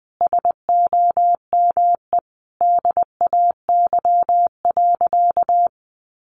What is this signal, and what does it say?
Morse code, heard as a keyed tone, about 20 wpm, 700 Hz. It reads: SOME DAY.
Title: Sentences from Top 900 Words 20wpm